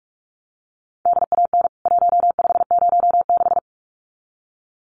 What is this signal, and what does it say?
Morse code, heard as a keyed tone, about 44 wpm, 700 Hz. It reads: DAN 1506
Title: CWT Exchange 45wpm